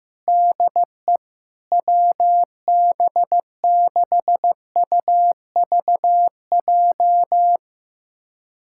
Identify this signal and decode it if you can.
Morse code, heard as a keyed tone, about 15 wpm, 700 Hz. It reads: DE WB6UVJ
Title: USA Callsign DE Callsign - 6 Characters - 15wpm